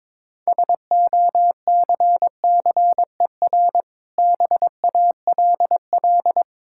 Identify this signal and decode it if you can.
Morse code, heard as a keyed tone, about 22 wpm, 690 Hz. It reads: SOCCER BALL
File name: Difficult Words with Hints - 22wpm